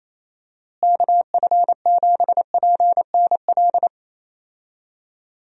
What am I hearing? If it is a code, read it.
Morse code, heard as a keyed tone, about 28 wpm, 700 Hz. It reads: KF7PNL
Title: USA Call Signs - 6 Characters - Repeated 2x - 28wpm